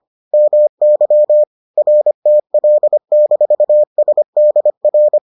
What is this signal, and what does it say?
Morse code, heard as a keyed tone, about 25 wpm, 600 Hz. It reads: MY RTL-SDR